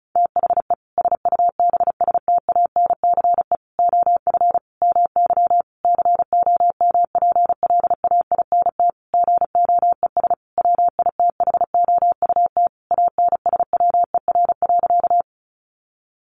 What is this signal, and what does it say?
Morse code, heard as a keyed tone, about 34 wpm, 700 Hz. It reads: THE SUBSTANCE OF MY COMPLAINT GOES WITHOUT ANSWER.
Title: Sentences from Top 1500 Words 35wpm